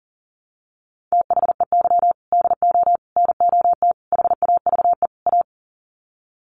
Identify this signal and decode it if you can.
Morse code, heard as a keyed tone, about 40 wpm, 700 Hz. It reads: THEY DO NOT HAVE A
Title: Sets of 5 Words 40wpm